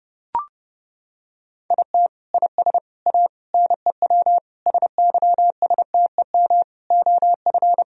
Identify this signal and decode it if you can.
Morse code, heard as a keyed tone, about 32 wpm, 700 Hz. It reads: IT IS A NEW SYSTEM OF
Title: Sentences from Top 1000 Words - Review Part 3 - 30wpm